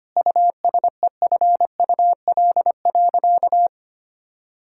Morse code, heard as a keyed tone, about 25 wpm, 700 Hz. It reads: USEFUL.
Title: Sentences from Top 1000 Words - Review Part 5 - 25wpm